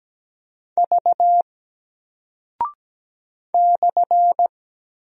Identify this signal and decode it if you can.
Morse code, heard as a keyed tone, about 17 wpm, 700 Hz. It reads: V /
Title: Single Character (Letter:Number:Punctuation:Prosign) Alt 17wpm